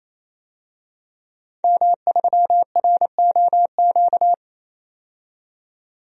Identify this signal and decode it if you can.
Morse code, heard as a keyed tone, about 28 wpm, 700 Hz. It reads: M3ROQ
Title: DXCC-100 Call Signs - 5 Characters - 28wpm